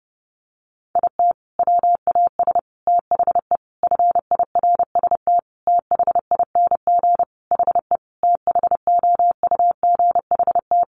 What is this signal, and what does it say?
Morse code, heard as a keyed tone, about 30 wpm, 700 Hz. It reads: IT WAS THE FIRST THING HE THOUGHT
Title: Sentences from Top 300 Words 30wpm